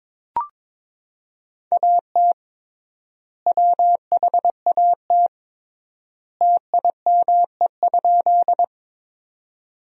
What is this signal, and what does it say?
Morse code, heard as a keyed tone, about 22 wpm, 700 Hz. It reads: AT WHAT TIME?
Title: 3x Word-Spacing - Sentences from Top 100 Words - 22wpm